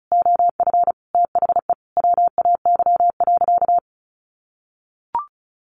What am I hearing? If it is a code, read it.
Morse code, heard as a keyed tone, about 34 wpm, 700 Hz. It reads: OF THE WAY.
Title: Sentences from Top 100 Words 35wpm